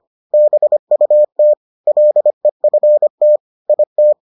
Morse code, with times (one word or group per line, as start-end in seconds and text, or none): 0.3-1.5 BUT
1.9-3.4 LEFT
3.7-4.1 IT